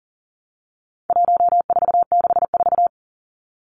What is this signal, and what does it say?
Morse code, heard as a keyed tone, about 40 wpm, 700 Hz. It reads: 1464
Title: Numbers - 4-Digit 40wpm